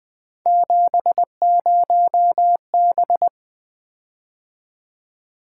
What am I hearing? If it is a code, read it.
Morse code, heard as a keyed tone, about 20 wpm, 700 Hz. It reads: 70B
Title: Three Letter-Number Combinations - 20wpm